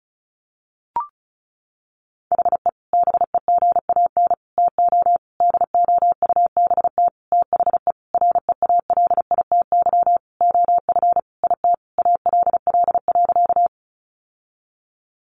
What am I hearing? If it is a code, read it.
Morse code, heard as a keyed tone, about 34 wpm, 700 Hz. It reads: HE BEGAN TO DOUBT THE REALITY OF IT ALL.
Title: Sentences from Top 1500 Words 35wpm